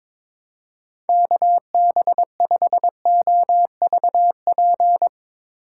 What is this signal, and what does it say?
Morse code, heard as a keyed tone, about 22 wpm, 700 Hz. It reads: KB5OVP
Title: Speed Racing - USA 2x3 Call Signs 22wpm